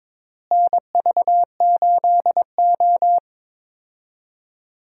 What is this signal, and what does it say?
Morse code, heard as a keyed tone, about 22 wpm, 700 Hz. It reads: NV8O
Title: USA Call Signs - 4 Characters - Repeated 3x - 22wpm